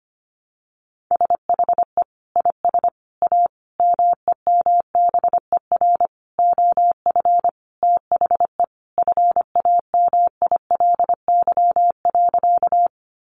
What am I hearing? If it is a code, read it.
Morse code, heard as a keyed tone, about 25 wpm, 700 Hz. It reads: SHE IS A MEMBER OF THE FAMILY.
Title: Sentences from Top 1100 Words 25wpm